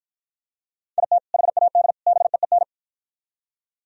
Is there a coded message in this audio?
Yes: Morse code from 1.0 to 2.6 s, heard as a keyed tone, about 52 wpm, 700 Hz. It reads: IT HAD BEEN